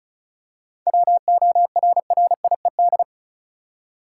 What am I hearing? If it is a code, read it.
Morse code, heard as a keyed tone, about 34 wpm, 700 Hz. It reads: WORRIED
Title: Seven Letter Words - 35wpm